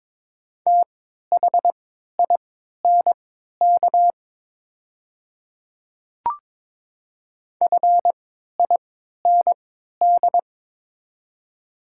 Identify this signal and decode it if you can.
Morse code, heard as a keyed tone, about 22 wpm, spaced as for 12 wpm, 690 Hz. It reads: THINK FIND
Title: 3x Character Spacing - Top 500 Words - 22wpm